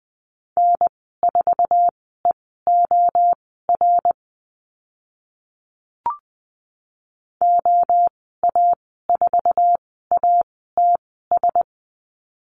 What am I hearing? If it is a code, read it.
Morse code, heard as a keyed tone, about 20 wpm, spaced as for 15 wpm, 700 Hz. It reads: N4EOR OA4ATS
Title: 062 Lesson 11g - Letter h - Call Signs 20wpm